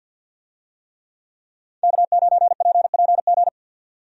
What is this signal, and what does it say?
Morse code, heard as a keyed tone, about 50 wpm, 700 Hz. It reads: K9PPG
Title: USA Call Signs - 5 Characters - 50wpm